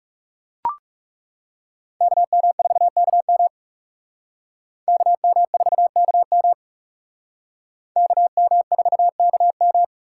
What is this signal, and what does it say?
Morse code, heard as a keyed tone, about 37 wpm, 700 Hz. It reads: KM4KM KM4KM KM4KM
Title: Speed Racing - USA Call Signs 30wpm